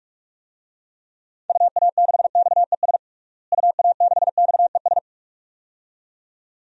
Morse code, heard as a keyed tone, about 42 wpm, 700 Hz. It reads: UA6XES UA6XES
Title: DXCC-100 Call Signs - 6 Characters - Repeated 2x - 45wpm